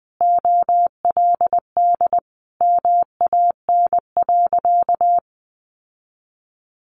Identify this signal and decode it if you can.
Morse code, heard as a keyed tone, about 20 wpm, 700 Hz. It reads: OLD MAN.
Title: Sentences from Top 800 Words 20wpm